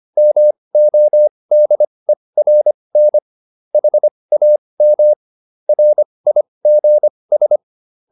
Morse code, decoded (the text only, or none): MODERN HAM RIGS